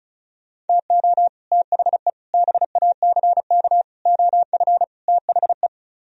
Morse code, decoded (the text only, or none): TO THE BACK OF THE